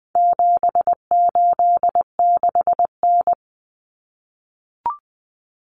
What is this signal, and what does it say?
Morse code, heard as a keyed tone, about 20 wpm, 700 Hz. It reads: 786N
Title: Common Cut Numbers - 4-Digit 20wpm